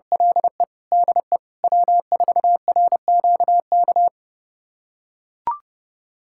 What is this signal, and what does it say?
Morse code, heard as a keyed tone, about 30 wpm, 700 Hz. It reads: LE DE W4RQK
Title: USA Callsign DE Callsign - 5 Characters - 30wpm